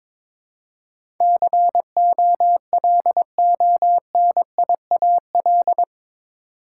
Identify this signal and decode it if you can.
Morse code, heard as a keyed tone, about 21 wpm, 700 Hz. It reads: COLONIAL
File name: practice / 22wpm / Eight Letter Words - 22wpm